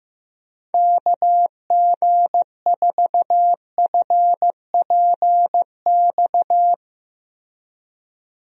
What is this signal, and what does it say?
Morse code, heard as a keyed tone, about 15 wpm, 700 Hz. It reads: KG4FPX
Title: USA Call Signs - 6 Characters - Repeated 3x - 15wpm